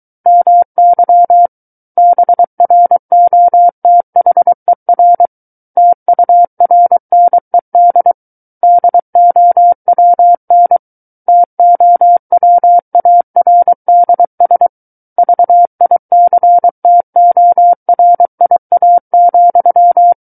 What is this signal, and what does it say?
Morse code, heard as a keyed tone, about 23 wpm, 700 Hz. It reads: MY BROTHER TURNED DOWN TOWARDS VICTORIA,
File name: Book One / 14-Chapter 14 - 23 WPM